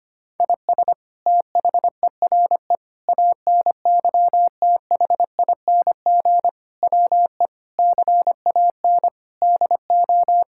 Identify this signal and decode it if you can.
Morse code, heard as a keyed tone, about 25 wpm, 700 Hz. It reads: IS THERE ANYTHING WE CAN DO
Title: Sentences from Top 1000 Words - Review Part 6 - 25wpm